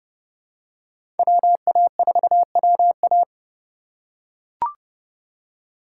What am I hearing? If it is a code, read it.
Morse code, heard as a keyed tone, about 30 wpm, 700 Hz. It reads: WA4WA
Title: USA Call Signs - 5 Characters - 2x Character Spacing - 30wpm